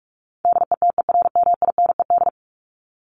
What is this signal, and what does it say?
Morse code, heard as a keyed tone, about 44 wpm, 700 Hz. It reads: DETERMINED
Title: ED Suffix - Top 100 Words with Base Words 45wpm